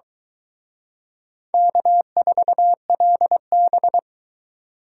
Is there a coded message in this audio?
Yes: Morse code from 1.5 to 4.0 s, heard as a keyed tone, about 24 wpm, 700 Hz. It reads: K4LB